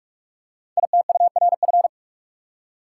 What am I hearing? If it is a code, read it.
Morse code, heard as a keyed tone, about 44 wpm, 690 Hz. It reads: ITURF
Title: Code Groups - 5 Letters - 45wpm